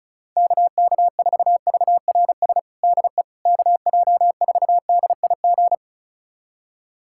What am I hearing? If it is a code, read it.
Morse code, heard as a keyed tone, about 34 wpm, 700 Hz. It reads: KK4VRS DE KJ4DIG
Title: USA Callsign DE Callsign - 6 Characters - 35wpm